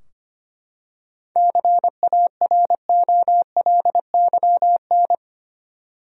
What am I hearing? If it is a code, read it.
Morse code, heard as a keyed tone, about 25 wpm, 700 Hz. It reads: CAROLYN
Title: USA Common Names - 25wpm